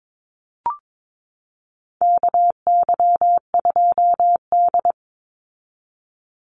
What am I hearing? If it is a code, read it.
Morse code, heard as a keyed tone, about 22 wpm, 700 Hz. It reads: KY2D
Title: Call Signs - 22wpm